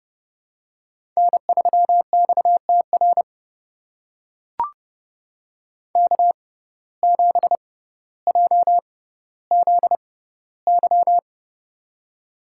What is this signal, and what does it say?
Morse code, heard as a keyed tone, about 30 wpm, spaced as for 15 wpm, 700 Hz. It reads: N3XTR K7JZY
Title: USA Call Signs - 5 Characters - 6x Character Spacing - 30wpm